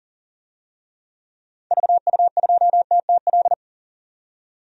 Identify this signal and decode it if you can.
Morse code, heard as a keyed tone, about 40 wpm, 700 Hz. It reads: VU2TTL
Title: DXCC-100 Call Signs - 6 Characters - 40wpm